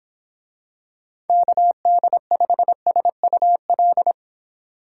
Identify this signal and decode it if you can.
Morse code, heard as a keyed tone, about 26 wpm, 700 Hz. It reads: KD5SUL